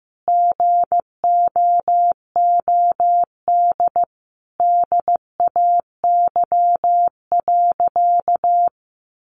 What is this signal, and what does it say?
Morse code, heard as a keyed tone, about 15 wpm, 700 Hz. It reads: GOOD DAY.
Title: Sentences from Top 600 Words 15wpm